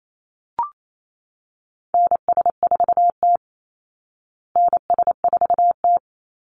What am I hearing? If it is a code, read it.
Morse code, heard as a keyed tone, about 28 wpm, 690 Hz. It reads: NS4T NS4T
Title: Call Signs Repeated 3x - 28wpm